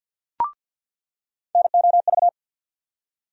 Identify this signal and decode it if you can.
Morse code, heard as a keyed tone, about 47 wpm, 700 Hz. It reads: NOV
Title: USA Call Sign Postfixes - 3 Letters 50wpm